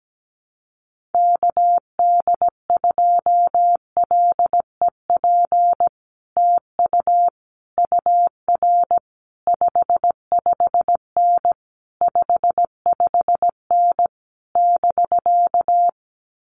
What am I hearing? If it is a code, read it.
Morse code, heard as a keyed tone, about 17 wpm, 700 Hz. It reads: KD2LEP TU UR 55N 55N <BK>
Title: Common QSO Phrases Encore 17wpm